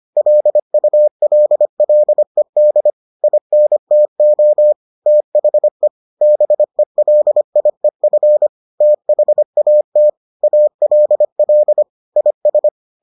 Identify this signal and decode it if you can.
Morse code, heard as a keyed tone, about 25 wpm, 600 Hz. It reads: LULLED INTO THE BELIEF THAT ALL IS